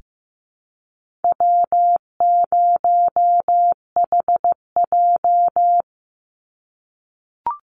Morse code, heard as a keyed tone, about 15 wpm, 700 Hz. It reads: W0HJ
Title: USA Call Signs 15wpm